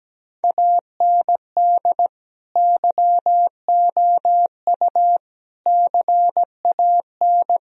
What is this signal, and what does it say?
Morse code, heard as a keyed tone, about 17 wpm, 700 Hz. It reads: AND YOU CAN
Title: Sentences from Top 1000 Words - Review Part 2 - 17wpm